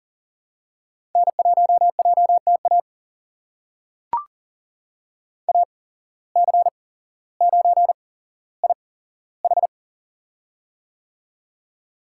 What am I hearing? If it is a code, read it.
Morse code, heard as a keyed tone, about 40 wpm, spaced as for 17 wpm, 700 Hz. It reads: N1JTA AC9IH
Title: USA Call Signs - 5 Characters - 8x Character Spacing - 40wpm